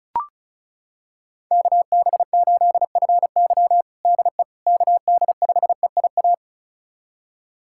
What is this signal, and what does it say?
Morse code, heard as a keyed tone, about 34 wpm, 700 Hz. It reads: KB8FY DE KD5EIA
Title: USA Callsign DE Callsign 35wpm